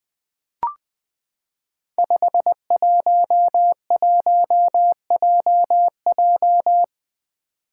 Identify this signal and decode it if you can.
Morse code, heard as a keyed tone, about 20 wpm, 700 Hz. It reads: 511JJ
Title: Five Challenging Letter-Number Combinations 20wpm